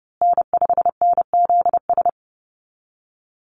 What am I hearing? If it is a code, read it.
Morse code, heard as a keyed tone, about 30 wpm, 700 Hz. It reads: N5NZS
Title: USA Call Signs Repeated 3x 30wpm